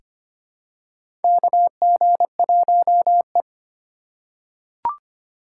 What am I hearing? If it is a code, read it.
Morse code, heard as a keyed tone, about 25 wpm, 700 Hz. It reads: KG1E
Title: USA Call Signs - 4 Characters - 3x Character Spacing - 25wpm